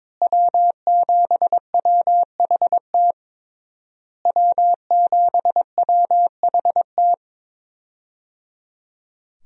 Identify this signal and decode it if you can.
Morse code, heard as a keyed tone, about 22 wpm, 700 Hz. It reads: W7WHT W7WHT
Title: USA Call Signs - 5 Characters - Repeated 3x - 22wpm